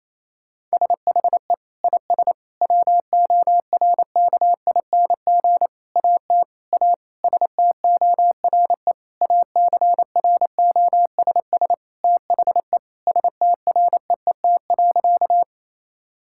Morse code, heard as a keyed tone, about 28 wpm, 700 Hz. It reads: SHE IS WORKING AT A STORE ACROSS THE STREET.
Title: Sentences from Top 1000 Words - Review Part 5 - 28wpm